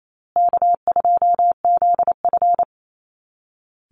Morse code, heard as a keyed tone, about 28 wpm, 700 Hz. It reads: K2ZF
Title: Easy Call Signs Repeated 3x - 28wpm